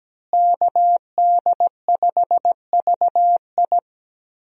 Morse code, heard as a keyed tone, about 17 wpm, 700 Hz. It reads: KD5VI
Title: USA Call Signs - 5 Characters - Repeated 3x - 17wpm